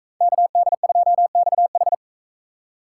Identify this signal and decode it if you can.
Morse code, heard as a keyed tone, about 44 wpm, 700 Hz. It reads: KD2XH